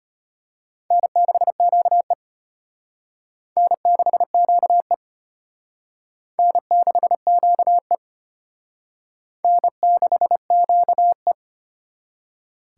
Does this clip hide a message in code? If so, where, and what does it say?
Yes: Morse code from 0.9 to 11.3 s, heard as a keyed tone, about 31 wpm, 700 Hz. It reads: N6QE N6QE N6QE N6QE